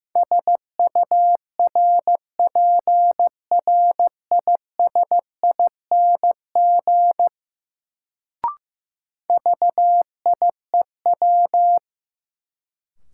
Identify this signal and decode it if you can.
Morse code, heard as a keyed tone, about 15 wpm, 700 Hz. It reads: SURPRISING VIEW
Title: Sets of 5 Words - New Word Drill - 15wpm